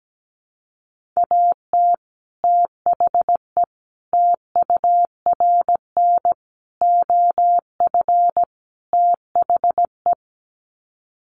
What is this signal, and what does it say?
Morse code, heard as a keyed tone, about 17 wpm, 700 Hz. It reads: AT THE TURN OF THE
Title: Sets of 5 Words 17wpm